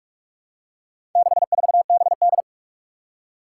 Morse code, heard as a keyed tone, about 47 wpm, 700 Hz. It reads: 64BD
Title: Four Challenging Letter-Number Combinations 45wpm